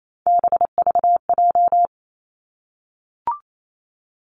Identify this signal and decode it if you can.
Morse code, heard as a keyed tone, about 28 wpm, 700 Hz. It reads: BVJ
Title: Three Challenging Letter-Number Combinations 28wpm